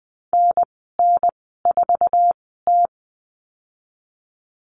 Morse code, heard as a keyed tone, about 20 wpm, spaced as for 14 wpm, 700 Hz. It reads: NN4T
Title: 048 Lesson 9g - Number 4 - Call Signs 20wpm